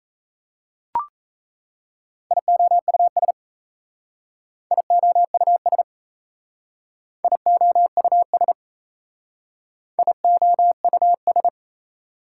Morse code, heard as a keyed tone, about 36 wpm, 700 Hz. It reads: IOUS IOUS IOUS IOUS